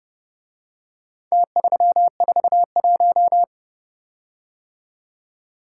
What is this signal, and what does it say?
Morse code, heard as a keyed tone, about 29 wpm, 700 Hz. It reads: T341